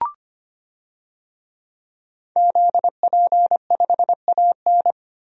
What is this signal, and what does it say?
Morse code, heard as a keyed tone, about 25 wpm, 700 Hz. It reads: ZP5AN
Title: DXCC-100 Call Signs - 5 Characters - Repeated 3x - 25wpm